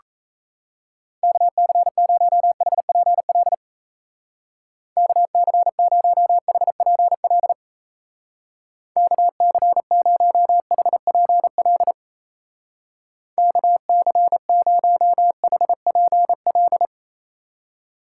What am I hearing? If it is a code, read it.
Morse code, heard as a keyed tone, about 35 wpm, 700 Hz. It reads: KC0HPL KC0HPL KC0HPL KC0HPL
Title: Speed Racing - USA 2x3 Call Signs 28wpm